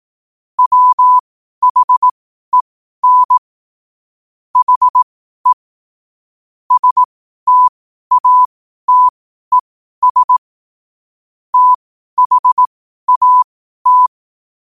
Morse code, heard as a keyed tone, about 18 wpm, spaced as for 12 wpm, 1000 Hz. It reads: WHEN HE STATES THAT
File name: qotdfiles / 12wpm